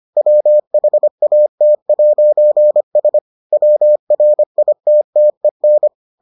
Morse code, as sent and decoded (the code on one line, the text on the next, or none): .-- .... .- - .----. ... / .-- .-. .. - - . -.
WHAT'S WRITTEN